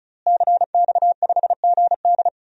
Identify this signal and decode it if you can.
Morse code, heard as a keyed tone, about 36 wpm, 700 Hz. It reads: CX5GD
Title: DXCC-100 Call Signs - 5 Characters - Repeated 2x - 35wpm